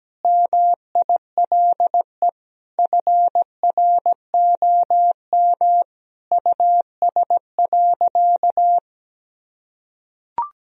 Morse code, heard as a keyed tone, about 17 wpm, 700 Hz. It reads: MILE FROM US.
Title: Sentences from Top 1000 Words - Review Part 1 - 17wpm